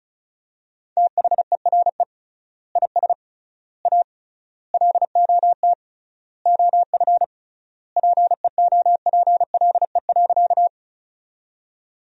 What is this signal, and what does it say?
Morse code, heard as a keyed tone, about 36 wpm, 700 Hz. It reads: THERE IS A LOT OF PEOPLE.
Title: 3x Word-Spacing - Sentences from Top 500 Words - 35wpm